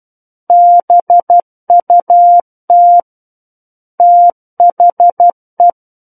Morse code, heard as a keyed tone, about 12 wpm, 700 Hz. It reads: BUT THE
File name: Book One / 03-Chapter 3 - 12 WPM